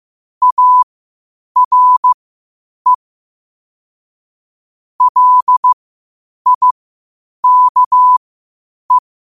Morse code, heard as a keyed tone, about 15 wpm, spaced as for 8 wpm, 1000 Hz. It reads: ARE LIKE
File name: qotdfiles / 8wpm